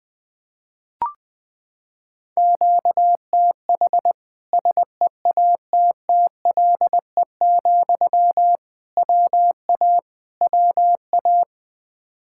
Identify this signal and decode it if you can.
Morse code, heard as a keyed tone, about 20 wpm, 700 Hz. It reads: QTH SEATTLE, WA WA
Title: Practice QSO - Exchange 1 - Beginning - 20wpm